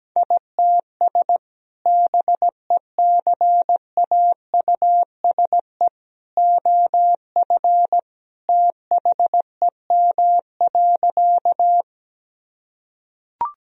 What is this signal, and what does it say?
Morse code, heard as a keyed tone, about 17 wpm, 700 Hz. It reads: ITS BECAUSE OF THEM.
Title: Sentences from Top 1000 Words - Review Part 3 - 17wpm